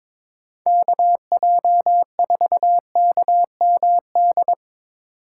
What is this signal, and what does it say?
Morse code, heard as a keyed tone, about 21 wpm, 700 Hz. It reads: KJ4KMD
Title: USA Call Signs - 6 Characters - 2x Character Spacing - 22wpm